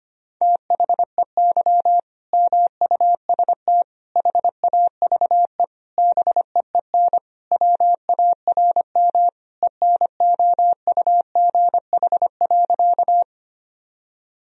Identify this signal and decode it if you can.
Morse code, heard as a keyed tone, about 25 wpm, 700 Hz. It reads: THEY MUST HAVE BEEN WARM ENOUGH.
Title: Sentences from Top 1000 Words - Review Part 2 - 25wpm